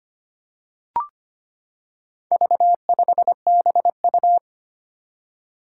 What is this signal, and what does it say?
Morse code, heard as a keyed tone, about 25 wpm, 700 Hz. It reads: V5BU
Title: Four Challenging Letter-Number Combinations 25wpm